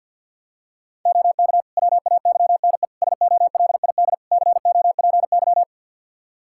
Morse code, heard as a keyed tone, about 49 wpm, 700 Hz. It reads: OK WAYNE SOLID COPY
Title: Practice QSO - Exchange 3 - More Info - 50wpm